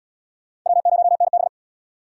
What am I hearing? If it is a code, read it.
Morse code, heard as a keyed tone, about 76 wpm, 700 Hz. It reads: W0NB